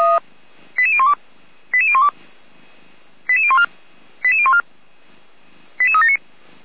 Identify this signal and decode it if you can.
Five-tone selective call (ZVEI), heard as quick runs of five tones, tones of about 70 ms. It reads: 88012 88012 88014 88014 88379
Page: Server at audio.f85360.feuerwehr-orlamuende.org